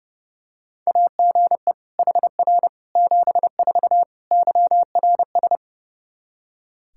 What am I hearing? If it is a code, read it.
Morse code, heard as a keyed tone, about 30 wpm, 700 Hz. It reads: AGE HR 74 YRS
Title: Practice QSO - Exchange 3 - More Info - 30wpm